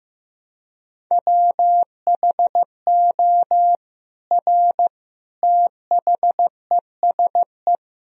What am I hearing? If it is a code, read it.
Morse code, heard as a keyed tone, about 16 wpm, 700 Hz. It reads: WHO R THESE
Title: Sentences from Top 200 Words and Abbreviations 15wpm